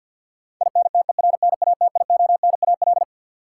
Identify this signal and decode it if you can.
Morse code, heard as a keyed tone, about 53 wpm, 700 Hz. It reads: INTERNATIONAL
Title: Sentences from Top 1000 Words - Review Part 4 - New Word Drill - 50wpm